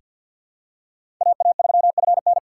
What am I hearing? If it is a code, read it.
Morse code, heard as a keyed tone, about 46 wpm, 700 Hz. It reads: AA3FN
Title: USA Call Signs - 5 Characters - 3x Character Spacing - 50wpm